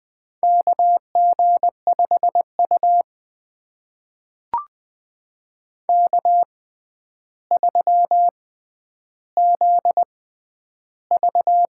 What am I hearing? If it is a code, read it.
Morse code, heard as a keyed tone, about 20 wpm, spaced as for 10 wpm, 700 Hz. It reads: KG5U K3ZV
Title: USA Call Signs - 6x Character Spacing - 20wpm